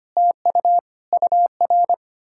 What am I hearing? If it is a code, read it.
Morse code, heard as a keyed tone, about 33 wpm, 700 Hz. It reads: TU UR